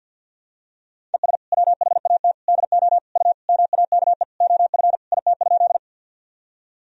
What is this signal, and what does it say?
Morse code, heard as a keyed tone, about 50 wpm, 700 Hz. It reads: ES WHAT DO U MAKE OF IT?
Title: Sentences from Top 100 Words and Abbreviations 50wpm